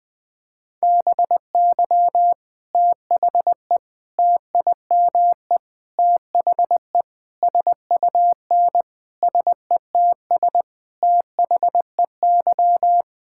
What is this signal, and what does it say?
Morse code, heard as a keyed tone, about 21 wpm, 700 Hz. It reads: BY THE TIME THE SUN SETS THEY
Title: Sentences with 4-Word Phrases - Spoken Beginning - 20wpm